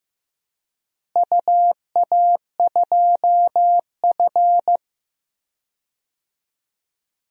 Morse code, heard as a keyed tone, about 15 wpm, 700 Hz. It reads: UA2F
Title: DXCC-100 Call Signs - 4 Characters - Repeated 3x - 15wpm